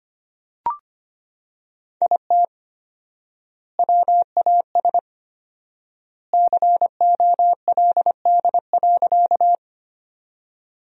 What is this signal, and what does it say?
Morse code, heard as a keyed tone, about 25 wpm, 700 Hz. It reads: IT WAS COLD.
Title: 4x Word-Spacing - Sentences from Top 400 Words - 25wpm